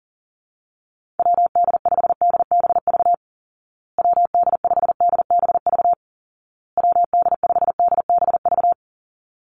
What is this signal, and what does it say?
Morse code, heard as a keyed tone, about 40 wpm, 700 Hz. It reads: WD5DBV WD5DBV WD5DBV
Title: Call Signs Repeated 2x - 40wpm